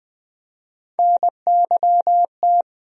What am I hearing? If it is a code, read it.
Morse code, heard as a keyed tone, about 20 wpm, 700 Hz. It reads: NYT
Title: USA Call Sign Postfixes - 3 Letters 20wpm